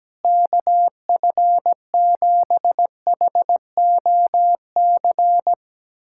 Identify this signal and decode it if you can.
Morse code, heard as a keyed tone, about 17 wpm, 700 Hz. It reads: KF7HOC
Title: USA Callsign DE Callsign 17wpm